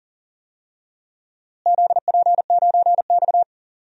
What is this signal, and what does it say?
Morse code, heard as a keyed tone, about 37 wpm, 700 Hz. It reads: ZP9X
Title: DXCC-100 Call Signs - 4 Characters - Repeated 3x - 40wpm